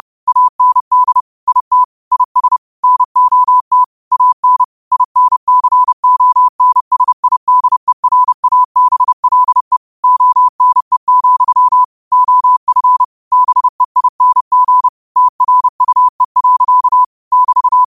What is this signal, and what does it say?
Morse code, heard as a keyed tone, about 30 wpm, 1000 Hz. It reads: AND IT IS NOT AN INCONSIDERABLE ONE, OF BEING TRUE. <BT>
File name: qotdfiles / 30wpm